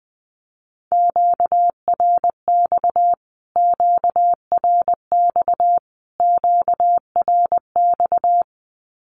Morse code, heard as a keyed tone, about 20 wpm, 700 Hz. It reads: QRX QRX QRX
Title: Q-Codes Repeated 3x 20wpm